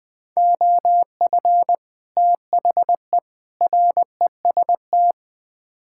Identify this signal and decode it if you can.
Morse code, heard as a keyed tone, about 20 wpm, 700 Hz. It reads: OF THE REST
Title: Build A Phrase - 5-Word Phrases - 20wpm